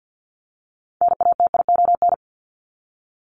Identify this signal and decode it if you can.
Morse code, heard as a keyed tone, about 48 wpm, 700 Hz. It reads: NATION
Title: ES:S Suffix - Top 100 Words with Base Words 50wpm